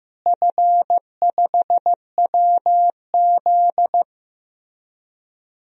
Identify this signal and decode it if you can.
Morse code, heard as a keyed tone, about 15 wpm, 700 Hz. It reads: F5WZ
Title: DXCC-100 Call Signs - 4 Characters - Repeated 3x - 15wpm